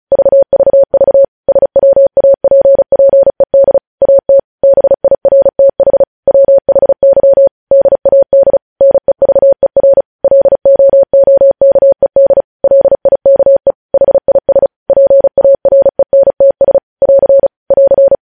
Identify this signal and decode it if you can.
Morse code, heard as a keyed tone, about 36 wpm, 550 Hz. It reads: VVV SWAPPED AT BIRTH WHY DAD NEVER LOOKED LIKE HIS PARENTS <AR> <AR>
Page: news headlines in morse code at 35 wpm